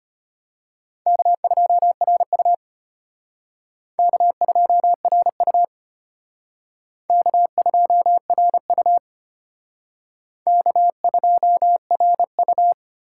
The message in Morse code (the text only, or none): K2RU K2RU K2RU K2RU